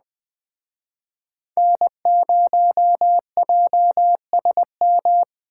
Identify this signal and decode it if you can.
Morse code, heard as a keyed tone, about 20 wpm, 700 Hz. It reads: N0JSM
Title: Speed Racing - USA Call Signs 17wpm